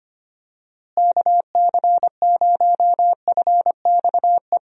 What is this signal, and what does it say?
Morse code, heard as a keyed tone, about 25 wpm, 700 Hz. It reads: KC0FXE
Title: USA Call Signs - 6 Characters - Repeated 2x - 25wpm